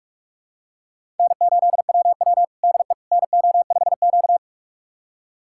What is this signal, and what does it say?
Morse code, heard as a keyed tone, about 45 wpm, 700 Hz. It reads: N8WW DE NO5Q
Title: USA Callsign DE Callsign - 4 Characters - 45wpm